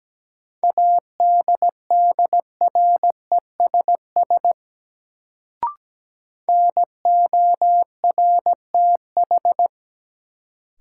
Morse code, heard as a keyed tone, about 17 wpm, 700 Hz. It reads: ADDRESS NORTH
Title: Sets of 5 Words Encore - New Word Drill - 17wpm